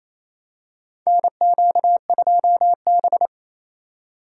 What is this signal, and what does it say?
Morse code, heard as a keyed tone, about 28 wpm, 700 Hz. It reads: NQ2B
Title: USA Call Signs - 4 Characters - 2x Character Spacing - 28wpm